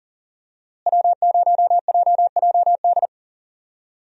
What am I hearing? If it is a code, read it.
Morse code, heard as a keyed tone, about 40 wpm, 700 Hz. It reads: W0JJD
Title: USA Call Signs - 5 Characters - Repeated 3x - 40wpm